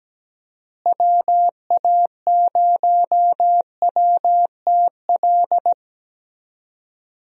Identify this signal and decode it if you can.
Morse code, heard as a keyed tone, about 17 wpm, 700 Hz. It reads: WA0WTL
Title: USA Call Signs - 6 Characters - Repeated 2x - 17wpm